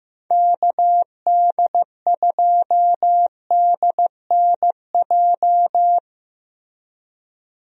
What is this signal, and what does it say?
Morse code, heard as a keyed tone, about 15 wpm, 700 Hz. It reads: KD2DNJ
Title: USA Call Signs - 6 Characters - Repeated 2x - 15wpm